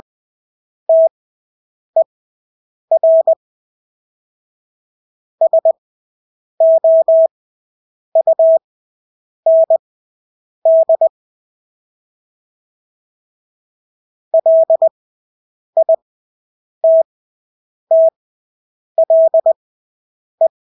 Morse code, heard as a keyed tone, about 20 wpm, spaced as for 8 wpm, 650 Hz. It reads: TER SOUND LITTLE
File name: Engelske ord 20/8 wpm